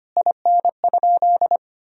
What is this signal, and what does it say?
Morse code, heard as a keyed tone, about 25 wpm, 700 Hz. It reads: IN?
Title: Sentences from Top 1000 Words - Review Part 2 - 25wpm